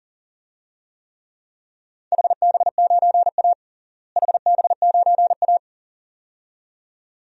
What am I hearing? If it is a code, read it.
Morse code, heard as a keyed tone, about 40 wpm, 700 Hz. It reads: HB9A HB9A
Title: DXCC-100 Call Signs - 4 Characters - Repeated 2x - 40wpm